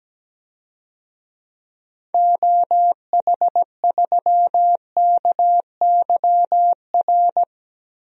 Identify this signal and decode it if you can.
Morse code, heard as a keyed tone, about 17 wpm, 700 Hz. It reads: OH3KYR
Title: DXCC-100 Call Signs - 6 Characters - 17wpm